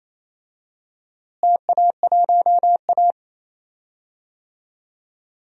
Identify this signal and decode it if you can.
Morse code, heard as a keyed tone, about 27 wpm, 700 Hz. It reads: TA1A
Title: DXCC-100 Call Signs - 4 Characters - Repeated 3x - 28wpm